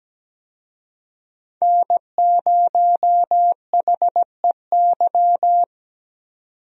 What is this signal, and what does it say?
Morse code, heard as a keyed tone, about 17 wpm, 700 Hz. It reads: N0HEY
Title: USA Call Signs - 5 Characters - 17wpm